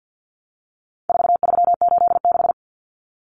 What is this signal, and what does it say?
Morse code, heard as a keyed tone, about 50 wpm, 700 Hz. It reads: 4386
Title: Numbers - 4-Digit 50wpm